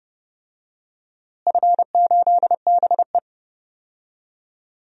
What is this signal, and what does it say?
Morse code, heard as a keyed tone, about 30 wpm, 700 Hz. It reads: F8BE
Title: DXCC-100 Call Signs - 4 Characters - Repeated 3x - 30wpm